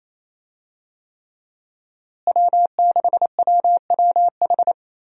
Morse code, heard as a keyed tone, about 28 wpm, 700 Hz. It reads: W6WWH
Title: USA Call Signs - 5 Characters - Repeated 3x - 28wpm